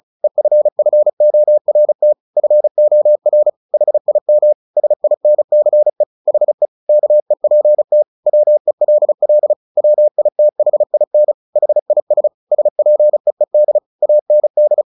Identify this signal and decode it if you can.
Morse code, heard as a keyed tone, about 34 wpm, 600 Hz. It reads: EFFORT FOR HIM SINCE HE KEPT WELL WITHIN HIS SPEED AND
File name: the old man and the sea / the old man and the sea 1 35wpm